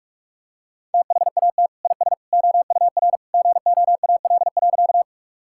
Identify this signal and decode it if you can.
Morse code, heard as a keyed tone, about 47 wpm, 700 Hz. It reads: THAT IS OUR GOAL.
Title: Sentences from Top 1000 Words - Review Part 5 - 45wpm